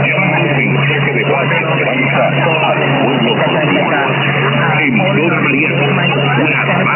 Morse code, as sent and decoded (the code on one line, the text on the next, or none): none